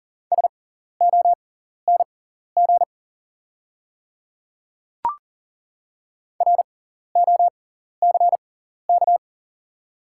Morse code, heard as a keyed tone, about 40 wpm, spaced as for 14 wpm, 700 Hz. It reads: SONG ROCK
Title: 6x Character Spacing - Top 500 Words - 40wpm